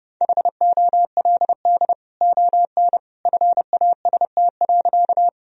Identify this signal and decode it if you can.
Morse code, heard as a keyed tone, about 30 wpm, 700 Hz. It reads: HOLD ON FAST.
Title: Sentences from Top 1000 Words - Review Part 2 - 30wpm